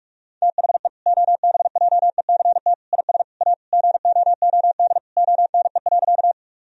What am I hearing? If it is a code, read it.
Morse code, heard as a keyed tone, about 41 wpm, 700 Hz. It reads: THE OBJECT IS A GOOD ONE.
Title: Sentences from Top 1000 Words - Review Part 3 - 45wpm